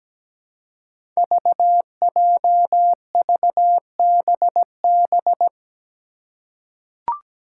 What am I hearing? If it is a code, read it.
Morse code, heard as a keyed tone, about 17 wpm, 700 Hz. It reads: VJVBB
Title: Five Challenging Letter-Number Combinations 17wpm